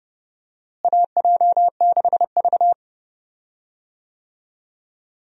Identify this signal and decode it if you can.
Morse code, heard as a keyed tone, about 30 wpm, 700 Hz. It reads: AJ6V
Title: USA Call Signs - 4 Characters - Repeated 3x - 30wpm